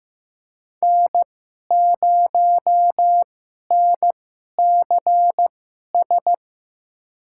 Morse code, heard as a keyed tone, about 15 wpm, spaced as for 11 wpm, 700 Hz. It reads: N0NCS
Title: USA Call Signs - 5 Characters - 2x Character Spacing - 15wpm